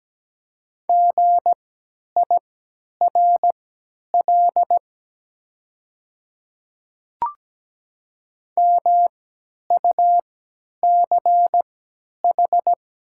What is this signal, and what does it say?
Morse code, heard as a keyed tone, about 17 wpm, spaced as for 10 wpm, 700 Hz. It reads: GIRL MUCH
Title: 3x Character Spacing - Top 500 Words - 17wpm